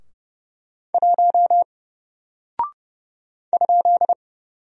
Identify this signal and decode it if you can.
Morse code, heard as a keyed tone, about 31 wpm, 700 Hz. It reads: 1 ?
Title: Single Character (Letter:Number:Punctuation:Prosign) 30wpm